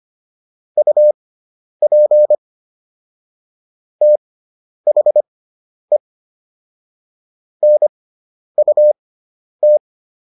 Morse code, heard as a keyed tone, about 25 wpm, spaced as for 10 wpm, 600 Hz. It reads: UP THE NUT